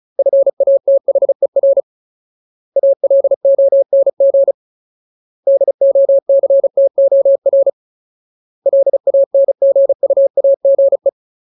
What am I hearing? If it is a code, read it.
Morse code, heard as a keyed tone, about 34 wpm, 550 Hz. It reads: FATHER ALONG DOCTOR LANGUAGE
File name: Ultimate Word List / 35WPM Ultimate